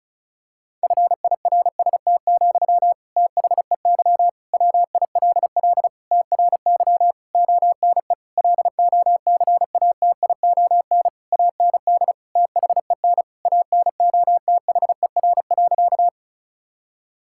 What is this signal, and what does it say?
Morse code, heard as a keyed tone, about 34 wpm, 700 Hz. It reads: FIRST, THEY WILL TRY ONE LOCATION AND THEN ANOTHER.
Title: Sentences from Top 1000 Words - Review Part 6 - 35wpm